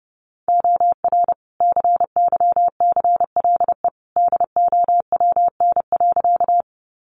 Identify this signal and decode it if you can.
Morse code, heard as a keyed tone, about 30 wpm, 700 Hz. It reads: OR CYCLE DOWN.
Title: Sentences from Top 900 Words 30wpm